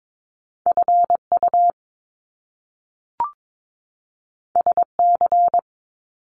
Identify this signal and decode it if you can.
Morse code, heard as a keyed tone, about 22 wpm, 700 Hz. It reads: FU SC
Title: Top Two Letter Combinations in Words 22wpm